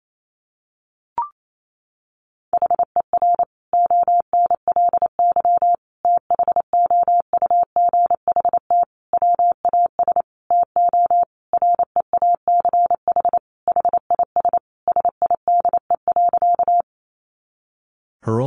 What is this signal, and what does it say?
Morse code, heard as a keyed tone, about 28 wpm, 700 Hz. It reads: HER ONLY THOUGHT WAS TO REACH HIS SIDE.
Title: Sentences from Top 400 Words 28wpm